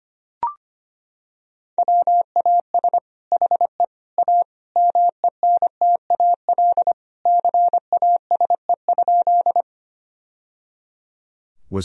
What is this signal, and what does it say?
Morse code, heard as a keyed tone, about 25 wpm, 700 Hz. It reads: WAS HE A MENTAL CASE?
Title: Sentences from Top 1000 Words - Review Part 4 - 25wpm